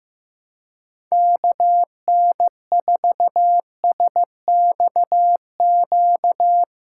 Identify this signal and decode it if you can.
Morse code, heard as a keyed tone, about 15 wpm, 700 Hz. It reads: KN4SXQ
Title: USA Callsign DE Callsign 15wpm